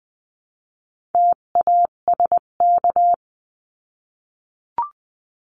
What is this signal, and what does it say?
Morse code, heard as a keyed tone, about 20 wpm, 700 Hz. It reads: TASK
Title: 258 Lesson 42c - Words 16-20wpm